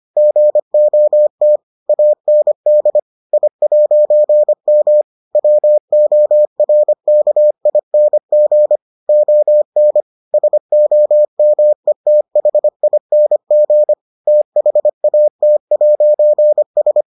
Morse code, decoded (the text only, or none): GOT AND I'M WORKING ON SOMETHING THAT'S